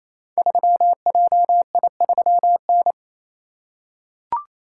Morse code, heard as a keyed tone, about 29 wpm, 700 Hz. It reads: 3JI3N
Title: Code Groups - 5 Letter:Numbers - 28wpm